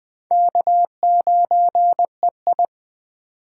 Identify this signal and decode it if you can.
Morse code, heard as a keyed tone, about 20 wpm, 700 Hz. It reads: K9EI
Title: Speed Racing - USA 1x2 Call Signs 17wpm